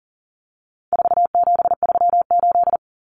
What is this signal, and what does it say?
Morse code, heard as a keyed tone, about 37 wpm, 700 Hz. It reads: 4738
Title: Numbers - 4-Digit 40wpm